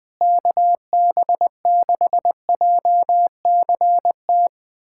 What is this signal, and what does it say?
Morse code, heard as a keyed tone, about 20 wpm, 700 Hz. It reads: KB6JCT
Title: USA Callsign DE Callsign - 6 Characters - 20wpm